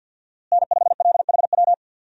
Morse code, heard as a keyed tone, about 52 wpm, 700 Hz. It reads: N5LHW
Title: USA Call Signs - 5 Characters - Repeated 3x - 50wpm